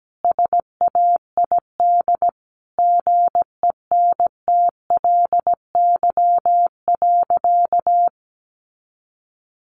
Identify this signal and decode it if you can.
Morse code, heard as a keyed tone, about 17 wpm, 700 Hz. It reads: SAID GENTLY.
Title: Sentences from Top 1500 Words 17wpm